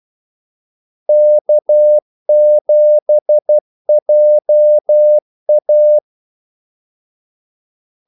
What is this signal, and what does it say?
Morse code, heard as a keyed tone, about 12 wpm, 600 Hz. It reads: K7JA
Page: 2 ekstra ordmellemrum mellem kaldesignalerne